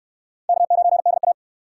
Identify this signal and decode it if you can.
Morse code, heard as a keyed tone, about 63 wpm, 700 Hz. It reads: K9DA